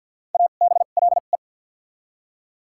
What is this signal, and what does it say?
Morse code, heard as a keyed tone, about 53 wpm, spaced as for 37 wpm, 700 Hz. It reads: ABLE
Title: 2x Character Spacing - Top 500 Words - 50wpm